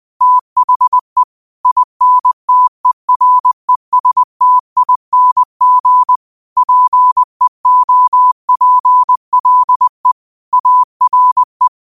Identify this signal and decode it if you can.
Morse code, heard as a keyed tone, about 20 wpm, 1000 Hz. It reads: THE INTERESTING PEOPLE ARE
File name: qotdfiles / 20wpm